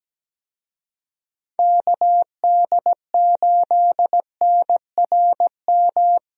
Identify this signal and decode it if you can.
Morse code, heard as a keyed tone, about 17 wpm, 700 Hz. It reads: KD8NRM
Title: USA Call Signs - 6 Characters - Repeated 2x - 17wpm